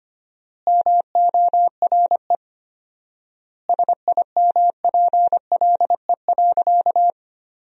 Morse code, heard as a keyed tone, about 25 wpm, 700 Hz. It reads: MORE SIMPLE.
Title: 4x Word-Spacing - Sentences from Top 400 Words - 25wpm